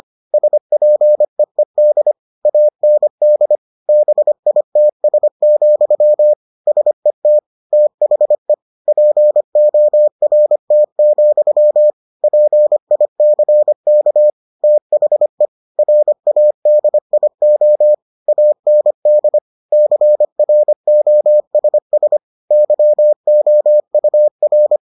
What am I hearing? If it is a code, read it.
Morse code, heard as a keyed tone, about 25 wpm, 600 Hz. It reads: SPEED AND BITS, SET THE PORT, PICK THE RADIO AND CROSS YOUR